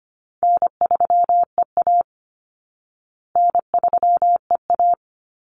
Morse code, heard as a keyed tone, about 25 wpm, 700 Hz. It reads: N3EA N3EA
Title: Easy Call Signs Repeated 2x - 25wpm